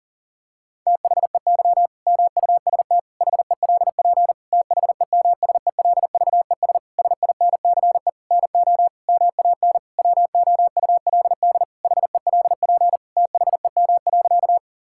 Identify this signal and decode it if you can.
Morse code, heard as a keyed tone, about 40 wpm, 700 Hz. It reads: THEY MUST HELP THEMSELVES SINCE NO MAN WOULD HELP THEM.
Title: Sentences from Top 1000 Words - Review Part 5 - 40wpm